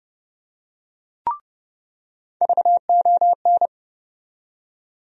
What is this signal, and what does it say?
Morse code, heard as a keyed tone, about 32 wpm, 700 Hz. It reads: VON
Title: USA Call Sign Postfixes - 3 Letters 30wpm